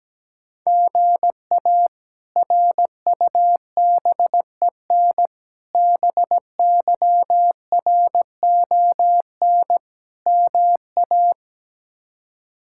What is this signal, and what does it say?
Morse code, heard as a keyed tone, about 17 wpm, 700 Hz. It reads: GA RUBEN BYRON MA
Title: K1USN SST QSO Practice - 17wpm